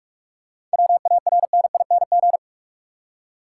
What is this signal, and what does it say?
Morse code, heard as a keyed tone, about 42 wpm, 700 Hz. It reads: WARNING
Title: Seven Letter Words - 45wpm